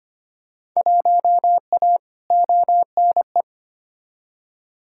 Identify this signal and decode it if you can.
Morse code, heard as a keyed tone, about 25 wpm, 700 Hz. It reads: 1A ONE
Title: Field Day Exchanges - 25wpm